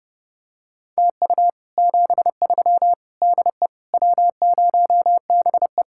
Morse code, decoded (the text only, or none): TU 73 DE W0BE